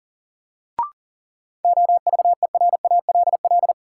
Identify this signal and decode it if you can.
Morse code, heard as a keyed tone, about 40 wpm, 700 Hz. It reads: OVERALL
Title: Sentences from Top 1000 Words - Review Part 7 - New Word Drill - 40wpm